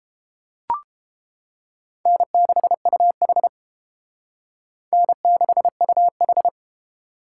Morse code, heard as a keyed tone, about 30 wpm, 700 Hz. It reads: N6UH N6UH